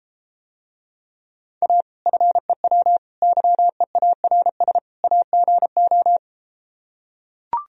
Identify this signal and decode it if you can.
Morse code, heard as a keyed tone, about 33 wpm, 700 Hz. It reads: A FEW YEARS AGO